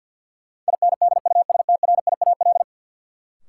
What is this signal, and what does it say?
Morse code, heard as a keyed tone, about 50 wpm, 700 Hz. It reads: INDUSTRIAL
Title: Sets of 5 Words - New Word Drill - 50wpm